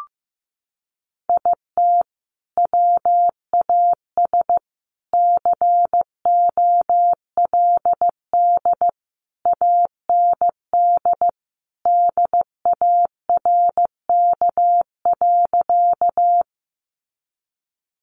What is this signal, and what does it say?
Morse code, heard as a keyed tone, about 15 wpm, 700 Hz. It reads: IT WAS COLD AND DARK.
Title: Sentences from Top 500 Words 15wpm